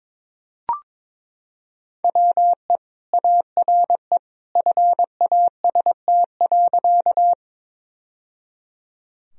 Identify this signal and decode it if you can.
Morse code, heard as a keyed tone, about 22 wpm, 700 Hz. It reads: WE ARE FAST.
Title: Sentences from Top 1000 Words - Review Part 2 - 22wpm